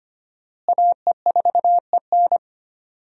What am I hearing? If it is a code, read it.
Morse code, heard as a keyed tone, about 25 wpm, 700 Hz. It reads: AE4EN
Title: USA Call Signs - 5 Characters - Repeated 3x - 25wpm